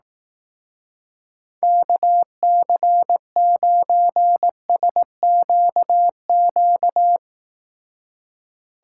Morse code, heard as a keyed tone, about 18 wpm, 700 Hz. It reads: KC9SQQ
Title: Speed Racing - USA 2x3 Call Signs 15wpm